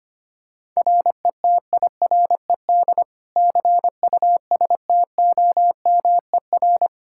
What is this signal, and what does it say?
Morse code, heard as a keyed tone, about 25 wpm, 700 Hz. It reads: RETIRED CUSTOMER
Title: Practice QSO - Exchange 3 - More Info - 25wpm